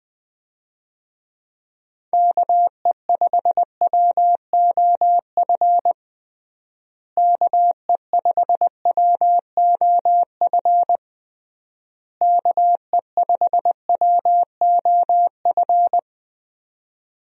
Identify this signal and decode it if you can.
Morse code, heard as a keyed tone, about 20 wpm, 700 Hz. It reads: KE5WOF KE5WOF KE5WOF
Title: USA Call Signs - 6 Characters - Repeated 3x - 20wpm